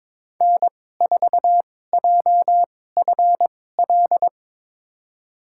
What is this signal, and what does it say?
Morse code, heard as a keyed tone, about 22 wpm, spaced as for 16 wpm, 700 Hz. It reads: N4JFL
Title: USA Call Signs - 5 Characters - 2x Character Spacing - 22wpm